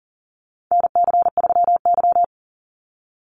Morse code, heard as a keyed tone, about 40 wpm, 700 Hz. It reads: NC3Y
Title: Call Signs Repeated 3x - 40wpm